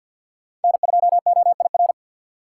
Morse code, heard as a keyed tone, about 52 wpm, 700 Hz. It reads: N2OIR
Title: USA Call Signs - 5 Characters - 4x Character Spacing - 50wpm